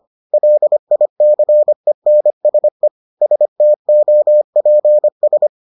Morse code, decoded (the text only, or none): LICENSE STOPS